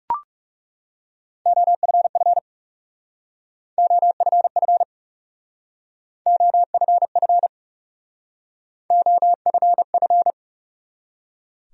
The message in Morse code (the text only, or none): OFF OFF OFF OFF